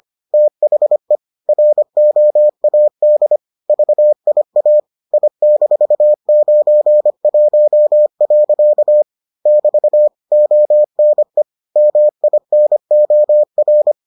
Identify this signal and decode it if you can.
Morse code, heard as a keyed tone, about 25 wpm, 600 Hz. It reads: THE ROAD VIA I-91. <BT> ONE MINOR